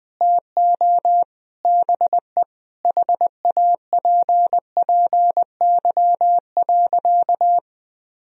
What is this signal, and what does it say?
Morse code, heard as a keyed tone, about 20 wpm, 700 Hz. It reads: TO BE HAPPY.
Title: Sentences from Top 1000 Words - Review Part 4 - 20wpm